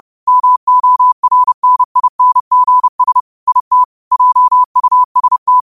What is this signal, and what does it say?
Morse code, heard as a keyed tone, about 30 wpm, 1000 Hz. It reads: MORNINGS IT JUST